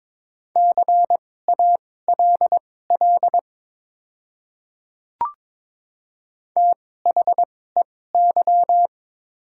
Morse code, heard as a keyed tone, about 23 wpm, spaced as for 16 wpm, 700 Hz. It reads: CALL THEY
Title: 2x Character Spacing - Top 100 Words - 22wpm